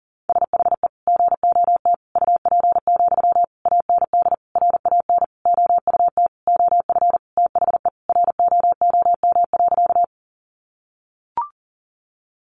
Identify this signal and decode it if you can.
Morse code, heard as a keyed tone, about 37 wpm, 700 Hz. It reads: SHE GOT UP, AND RAN OUT OF THE ROOM.
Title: Sentences from Top 500 Words 40wpm